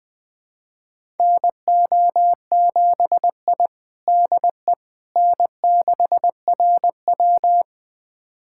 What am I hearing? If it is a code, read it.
Morse code, heard as a keyed tone, about 20 wpm, 700 Hz. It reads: NO7I DE N6RW
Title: USA Callsign DE Callsign - 4 Characters - 20wpm